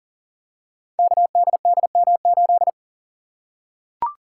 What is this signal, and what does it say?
Morse code, heard as a keyed tone, about 38 wpm, 700 Hz. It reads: KDDM8
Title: Code Groups - 5 Letter:Numbers - 40wpm